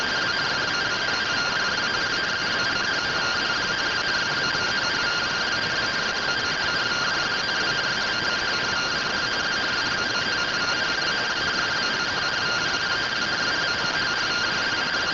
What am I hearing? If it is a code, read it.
Two-tone teleprinter signal, about 170 Hz shift.